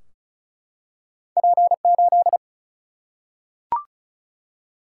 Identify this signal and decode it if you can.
Morse code, heard as a keyed tone, about 36 wpm, 700 Hz. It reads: P8
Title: Single-Letter Single-Number 35wpm